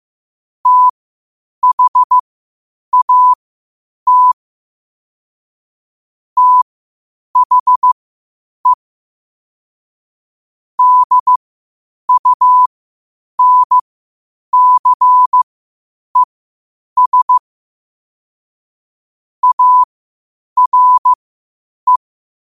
Morse code, heard as a keyed tone, about 15 wpm, spaced as for 8 wpm, 1000 Hz. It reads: THAT THE DUNCES ARE
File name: qotdfiles / 8wpm